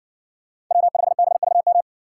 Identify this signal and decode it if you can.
Morse code, heard as a keyed tone, about 60 wpm, 700 Hz. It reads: W5BVM